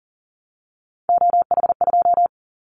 Morse code, heard as a keyed tone, about 42 wpm, 700 Hz. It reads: OH2
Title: Callsign Prefixes for Top 50 DXCC - 40wpm